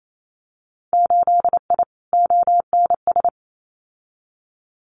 Morse code, heard as a keyed tone, about 28 wpm, 700 Hz. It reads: 8I ONS
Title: Winter Field Day Exchanges Encore 28wpm